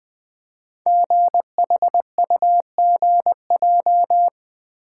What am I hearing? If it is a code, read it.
Morse code, heard as a keyed tone, about 20 wpm, 700 Hz. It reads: GHUGJ
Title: Code Groups - 5 Letters - 20wpm